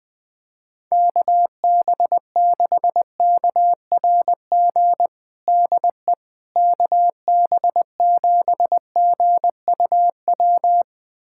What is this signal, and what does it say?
Morse code, heard as a keyed tone, about 20 wpm, 700 Hz. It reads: KB6KRG DE KB7GUW
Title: USA Callsign DE Callsign - 6 Characters - 20wpm